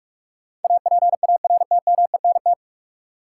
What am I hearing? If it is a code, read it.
Morse code, heard as a keyed tone, about 46 wpm, 700 Hz. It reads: APARTMENT
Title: Sentences from Top 900 Words - New Word Drill - 45wpm